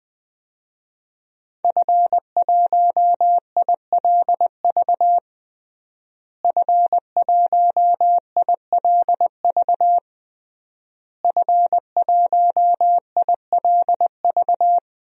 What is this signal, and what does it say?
Morse code, heard as a keyed tone, about 20 wpm, 700 Hz. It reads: F1ILV F1ILV F1ILV
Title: DXCC-100 Call Signs - 5 Characters - Repeated 3x - 20wpm